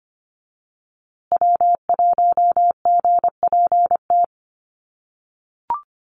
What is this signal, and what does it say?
Morse code, heard as a keyed tone, about 25 wpm, 700 Hz. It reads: W1GPT
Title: USA Call Signs Repeated 3x 25wpm